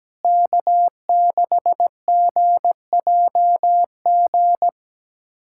Morse code, heard as a keyed tone, about 17 wpm, 700 Hz. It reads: K6GJG
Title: USA Callsign DE Callsign - 5 Characters - 17wpm